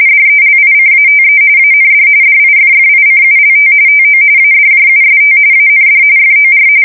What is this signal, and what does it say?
RTTY, about 45.45 baud. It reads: DELICATE AND DIFFICULT TO ADJUST AND FOR
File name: fsk170 / printing-telegraph